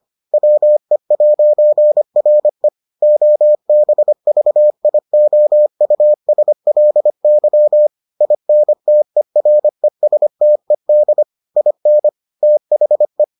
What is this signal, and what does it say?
Morse code, heard as a keyed tone, about 25 wpm, 600 Hz. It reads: WE'RE OBVIOUSLY INTERESTED IN THE